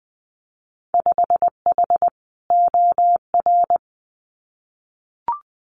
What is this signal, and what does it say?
Morse code, heard as a keyed tone, about 20 wpm, 700 Hz. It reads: 5H OR
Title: Winter Field Day Exchanges Encore 20wpm